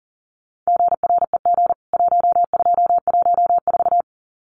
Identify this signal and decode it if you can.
Morse code, heard as a keyed tone, about 40 wpm, 700 Hz. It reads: GREG 1214
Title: CWT Exchange 40wpm